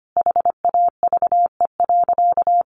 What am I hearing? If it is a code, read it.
Morse code, heard as a keyed tone, about 25 wpm, 700 Hz. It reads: HAVE.
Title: Sentences from Top 1200 Words 25wpm